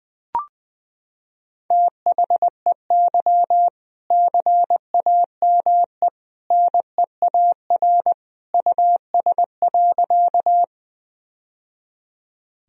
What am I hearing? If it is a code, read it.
Morse code, heard as a keyed tone, about 20 wpm, 700 Hz. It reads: THEY CAME NEAR US.
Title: Sentences from Top 1000 Words - Review Part 1 - 20wpm